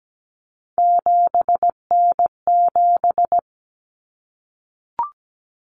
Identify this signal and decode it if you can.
Morse code, heard as a keyed tone, about 17 wpm, 700 Hz. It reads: 7N7
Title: Common Cut Numbers - 3-Digit 17wpm